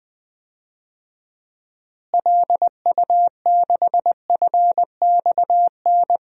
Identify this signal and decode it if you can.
Morse code, heard as a keyed tone, about 20 wpm, 700 Hz. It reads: LU6FXN
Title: DXCC-100 Call Signs - 6 Characters - Repeated 2x - 20wpm